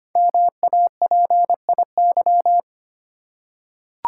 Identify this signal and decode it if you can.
Morse code, heard as a keyed tone, about 25 wpm, 700 Hz. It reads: MAPIY
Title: Code Groups - 5 Letters - 25wpm